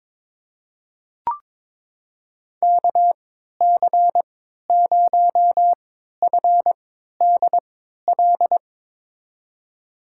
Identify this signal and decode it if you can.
Morse code, heard as a keyed tone, about 22 wpm, spaced as for 13 wpm, 700 Hz. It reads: KC0FDL
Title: USA Call Signs - 6 Characters - 3x Character Spacing - 22wpm